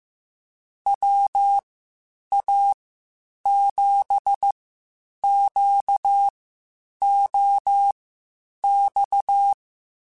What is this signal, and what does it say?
Morse code, heard as a keyed tone, about 15 wpm, spaced as for 8 wpm, 800 Hz. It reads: WA7QOX
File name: (spaced) / QSOs Spaced-mp3s / WA7QOX 2 @13 sp